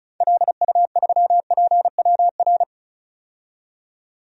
Morse code, heard as a keyed tone, about 36 wpm, 700 Hz. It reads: LU3PWR
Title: DXCC-100 Call Signs - 6 Characters - 35wpm